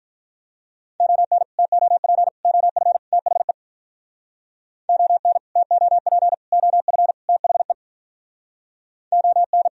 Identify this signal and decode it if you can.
Morse code, heard as a keyed tone, about 46 wpm, 700 Hz. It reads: ON TOP OF THE ON TOP OF THE ON